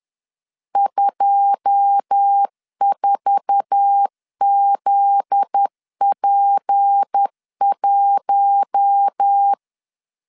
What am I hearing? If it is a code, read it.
Morse code, heard as a keyed tone, about 10 wpm, 800 Hz. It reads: 24ZP1
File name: LEZIONI / 10 SIGNS / 10A SLASH 10wpm 2808